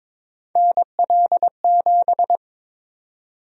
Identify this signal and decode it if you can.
Morse code, heard as a keyed tone, about 22 wpm, 700 Hz. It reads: NL7
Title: USA Call Sign Prefixes - 22wpm